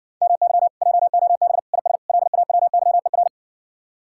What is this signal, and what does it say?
Morse code, heard as a keyed tone, about 67 wpm, 700 Hz. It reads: MY JOB IS LAWYER